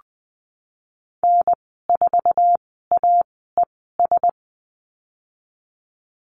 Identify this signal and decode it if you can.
Morse code, heard as a keyed tone, about 20 wpm, spaced as for 14 wpm, 700 Hz. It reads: N4AES
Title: 062 Lesson 11g - Letter h - Call Signs 20wpm